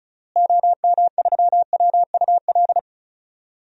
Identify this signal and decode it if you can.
Morse code, heard as a keyed tone, about 35 wpm, 700 Hz. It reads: OM3WUL
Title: DXCC-100 Call Signs - 6 Characters - Repeated 3x - 35wpm